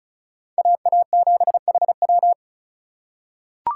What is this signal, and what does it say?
Morse code, heard as a keyed tone, about 34 wpm, 700 Hz. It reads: AA7HW
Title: USA Call Signs - 5 Characters - 6x Character Spacing - 35wpm